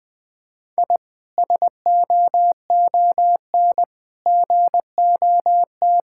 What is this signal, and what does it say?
Morse code, heard as a keyed tone, about 20 wpm, 700 Hz. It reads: I SOON GOT
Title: Sentences from Top 1000 Words - Review Part 6 - 20wpm